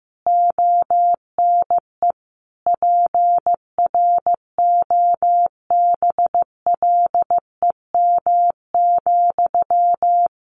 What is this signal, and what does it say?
Morse code, heard as a keyed tone, about 15 wpm, 700 Hz. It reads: ONE PROBLEM,
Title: Sentences from Top 1400 Words 15wpm